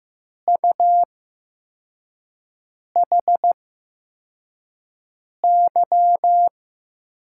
Morse code, heard as a keyed tone, about 15 wpm, spaced as for 4 wpm, 700 Hz. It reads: UHY
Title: USA Call Signs - 5 Characters - 8x Character Spacing - 15wpm